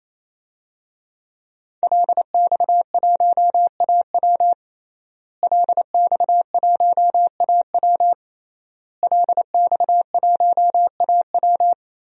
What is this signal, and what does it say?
Morse code, heard as a keyed tone, about 28 wpm, 700 Hz. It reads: LX1AW LX1AW LX1AW
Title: DXCC-100 Call Signs - 5 Characters - Repeated 3x - 28wpm